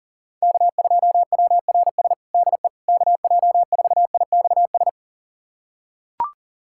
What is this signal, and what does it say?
Morse code, heard as a keyed tone, about 40 wpm, 700 Hz. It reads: K2WRS DE KJ4IXS
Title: USA Callsign DE Callsign 40wpm